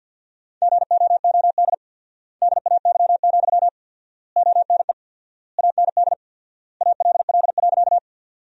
Morse code, heard as a keyed tone, about 50 wpm, 700 Hz. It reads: GOOD DAY, ONE AND ALL.
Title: 4x Word-Spacing - Sentences from Top 200 Words - 50wpm